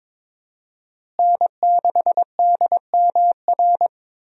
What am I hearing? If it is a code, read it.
Morse code, heard as a keyed tone, about 22 wpm, 700 Hz. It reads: N6DMR
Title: USA Call Signs - 5 Characters - Repeated 3x - 22wpm